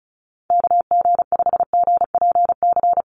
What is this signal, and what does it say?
Morse code, heard as a keyed tone, about 34 wpm, 700 Hz. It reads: KG5GPC
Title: USA Call Signs 35wpm